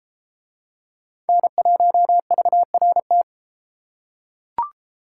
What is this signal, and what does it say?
Morse code, heard as a keyed tone, about 31 wpm, 700 Hz. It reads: N1VRT